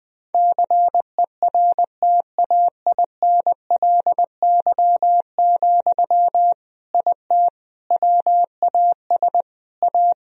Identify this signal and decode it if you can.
Morse code, heard as a keyed tone, about 20 wpm, 700 Hz. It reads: CERTAINLY, IT WAS A
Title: Sentences from Top 1000 Words - Review Part 2 - 20wpm